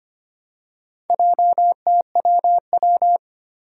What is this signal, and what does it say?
Morse code, heard as a keyed tone, about 25 wpm, 700 Hz. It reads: JTWW
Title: Four Challenging Letter-Number Combinations 25wpm